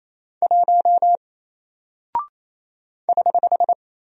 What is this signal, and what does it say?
Morse code, heard as a keyed tone, about 28 wpm, 700 Hz. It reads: 1 <HH>
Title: Single Character (Letter:Number:Punctuation:Prosign) Alt 28wpm